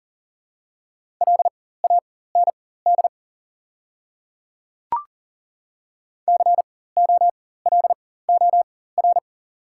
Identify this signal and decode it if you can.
Morse code, heard as a keyed tone, about 40 wpm, spaced as for 19 wpm, 700 Hz. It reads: LAND COLOR
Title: 4x Character Spacing - Top 300 Words - 40wpm